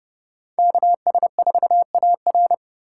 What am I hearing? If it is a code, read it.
Morse code, heard as a keyed tone, about 37 wpm, 700 Hz. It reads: KS4AR
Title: Speed Racing - USA 2x2 Call Signs 30wpm